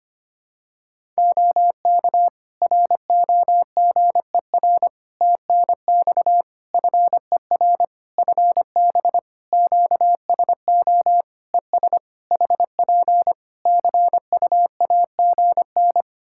Morse code, heard as a keyed tone, about 25 wpm, 700 Hz. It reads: OK ROGER TNX FER FB QSO ES HP CUAGN
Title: Practice QSO - Exchange 4 - Ending - 25wpm